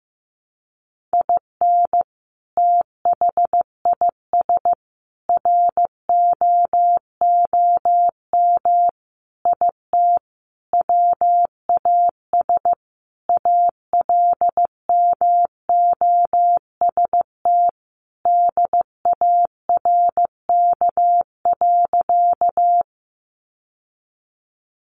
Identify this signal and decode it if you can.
Morse code, heard as a keyed tone, about 15 wpm, 700 Hz. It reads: IN THIS ROOM IT WAS ALMOST DARK.
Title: Sentences from Top 700 Words 15wpm